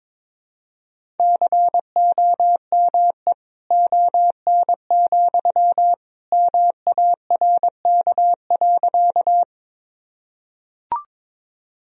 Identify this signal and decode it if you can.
Morse code, heard as a keyed tone, about 22 wpm, 700 Hz. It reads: COME ON, MARK.
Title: Sentences from Top 1000 Words - Review Part 1 - 22wpm